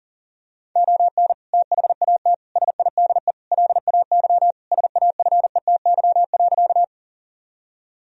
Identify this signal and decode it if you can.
Morse code, heard as a keyed tone, about 37 wpm, 700 Hz. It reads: ON THAT SIDE LAY SAFETY.
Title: Sentences from Top 1000 Words - Review Part 4 - 40wpm